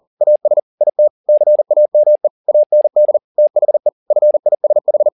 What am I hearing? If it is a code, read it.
Morse code, heard as a keyed tone, about 40 wpm, 600 Hz. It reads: AS IT CAME AND THE FISH